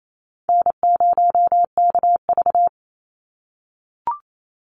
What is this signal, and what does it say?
Morse code, heard as a keyed tone, about 28 wpm, 700 Hz. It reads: N0KV
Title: Call Signs - 28wpm